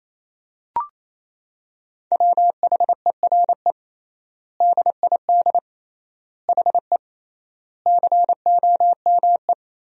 Morse code, heard as a keyed tone, about 28 wpm, 700 Hz. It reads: WHERE DID HE COME
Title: 3x Word-Spacing - Sentences from Top 200 Words - 28wpm